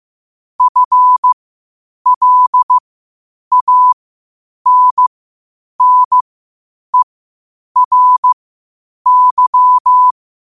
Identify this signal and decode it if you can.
Morse code, heard as a keyed tone, about 15 wpm, spaced as for 8 wpm, 1000 Hz. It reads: FLANNERY
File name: qotdfiles / 8wpm